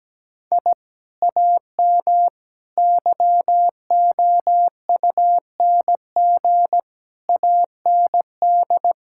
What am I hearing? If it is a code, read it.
Morse code, heard as a keyed tone, about 17 wpm, 700 Hz. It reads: I AM YOUNG AND
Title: Sentences from Top 1000 Words - Review Part 6 - 17wpm